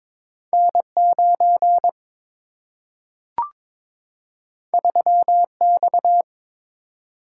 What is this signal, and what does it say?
Morse code, heard as a keyed tone, about 22 wpm, 700 Hz. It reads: N9 3X
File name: practice / 22wpm / Two Letter-Number Combinations - 22wpm